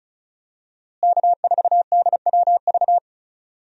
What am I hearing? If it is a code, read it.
Morse code, heard as a keyed tone, about 34 wpm, 700 Hz. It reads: K4DWV
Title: USA Call Signs - 5 Characters - Repeated 2x - 35wpm